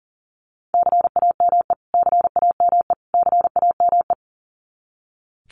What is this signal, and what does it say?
Morse code, heard as a keyed tone, about 40 wpm, 700 Hz. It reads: CAME CAME CAME
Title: Top 200 Words Repeated 40wpm